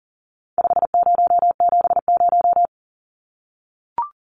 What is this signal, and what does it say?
Morse code, heard as a keyed tone, about 40 wpm, 700 Hz. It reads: 5070
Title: Numbers - 4-Digit 40wpm